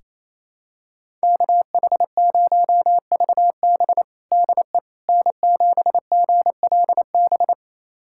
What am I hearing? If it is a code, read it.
Morse code, heard as a keyed tone, about 28 wpm, 700 Hz. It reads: KH0VB DE N7GLB
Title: USA Callsign DE Callsign - 5 Characters - 28wpm